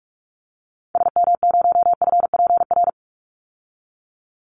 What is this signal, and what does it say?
Morse code, heard as a keyed tone, about 44 wpm, 700 Hz. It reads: SM0FPR
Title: Call Signs Repeated 3x - 45wpm